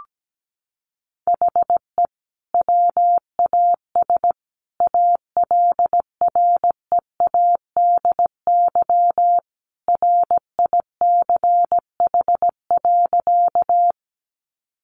Morse code, heard as a keyed tone, about 17 wpm, 700 Hz. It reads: HE WAS ALREADY RICH.
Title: Sentences from Top 1100 Words 17wpm